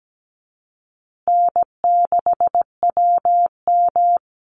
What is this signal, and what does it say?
Morse code, heard as a keyed tone, about 17 wpm, 700 Hz. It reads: N6WM
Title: Easy Call Signs Repeated 2x - 17wpm